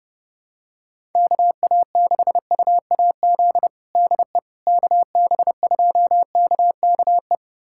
Morse code, heard as a keyed tone, about 32 wpm, 700 Hz. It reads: KA6UAZ DE KB2KKE
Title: USA Callsign DE Callsign - 6 Characters - 30wpm